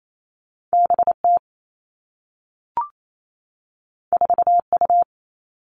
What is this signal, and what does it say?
Morse code, heard as a keyed tone, about 28 wpm, 700 Hz. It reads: BT 4U
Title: All Cut Numbers - 2-Digit 28wpm